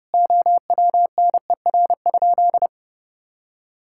Morse code, heard as a keyed tone, about 30 wpm, 700 Hz. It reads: OWNER?
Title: Sentences from Top 1600 Words 30wpm